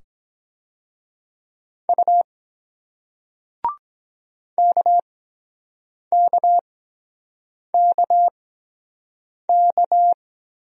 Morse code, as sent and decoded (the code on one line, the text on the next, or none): ..- / -.- -.- -.- -.-
U KKKK